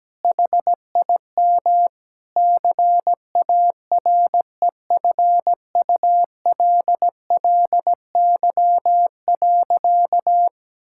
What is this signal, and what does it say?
Morse code, heard as a keyed tone, about 17 wpm, 700 Hz. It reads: HIM CAREFULLY.
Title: Sentences from Top 1600 Words 17wpm